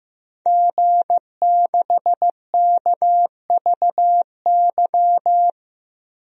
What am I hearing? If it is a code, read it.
Morse code, heard as a keyed tone, about 15 wpm, 700 Hz. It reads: G6KVY
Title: DXCC-100 Call Signs - 5 Characters - Repeated 2x - 15wpm